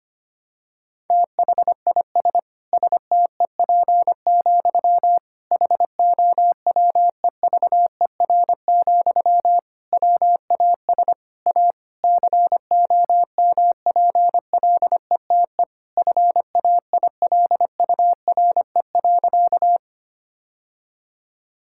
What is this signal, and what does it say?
Morse code, heard as a keyed tone, about 25 wpm, 700 Hz. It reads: THIS STEP, HOWEVER, WAS A COMPLETE FAILURE.
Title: Sentences from Top 1000 Words - Review Part 6 - 25wpm